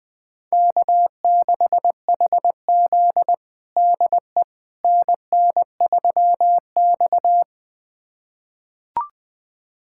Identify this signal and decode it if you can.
Morse code, heard as a keyed tone, about 21 wpm, 700 Hz. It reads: K6HZ DE NN3X
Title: USA Callsign DE Callsign - 4 Characters - 20wpm